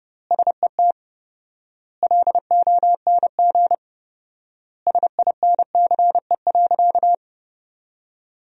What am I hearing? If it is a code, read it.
Morse code, heard as a keyed tone, about 32 wpm, 700 Hz. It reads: SET LONG SINCE.
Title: 4x Word-Spacing - Sentences from Top 300 Words - 30wpm